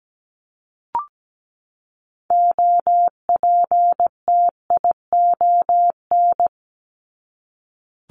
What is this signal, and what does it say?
Morse code, heard as a keyed tone, about 17 wpm, 700 Hz. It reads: OPTION
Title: Top 1001-1250 Words 17wpm